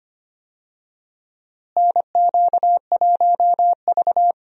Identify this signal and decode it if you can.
Morse code, heard as a keyed tone, about 25 wpm, 700 Hz. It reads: NQ1V
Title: USA Call Signs - 4 Characters - Repeated 3x - 25wpm